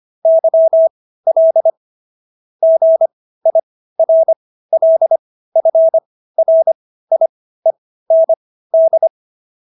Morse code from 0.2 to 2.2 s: YL

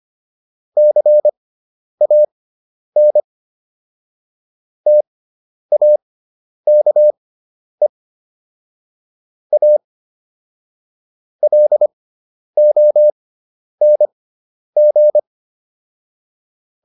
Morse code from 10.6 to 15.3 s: LONG

